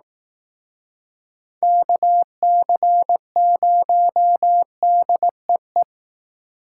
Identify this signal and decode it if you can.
Morse code, heard as a keyed tone, about 18 wpm, 700 Hz. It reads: KC0DEE
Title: Speed Racing - USA 2x3 Call Signs 15wpm